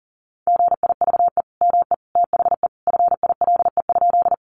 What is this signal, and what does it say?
Morse code, heard as a keyed tone, about 40 wpm, 700 Hz. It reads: GIVE ME THE FILE?
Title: Sentences from Top 1200 Words 40wpm